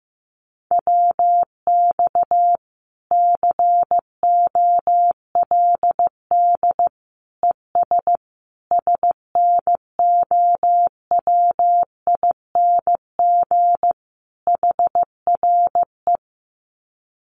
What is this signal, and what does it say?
Morse code, heard as a keyed tone, about 15 wpm, 700 Hz. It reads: WX COLD ES SNOWING HRE
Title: Common QSO Phrases 15wpm